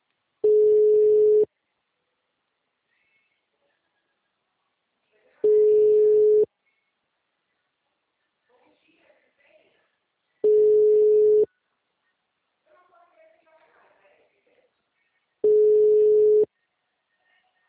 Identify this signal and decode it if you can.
Ringback tone (Europe, 425 Hz), with speech.
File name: out- -2002- - - .618